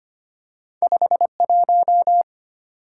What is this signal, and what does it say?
Morse code, heard as a keyed tone, about 25 wpm, 700 Hz. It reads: 51
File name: Two Letter-Number Combinations - 25wpm